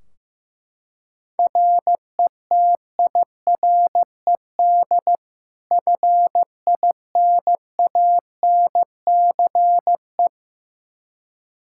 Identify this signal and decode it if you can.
Morse code, heard as a keyed tone, about 15 wpm, 700 Hz. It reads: RETIRED FINANCE
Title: Practice QSO - Exchange 3 - More Info - 15wpm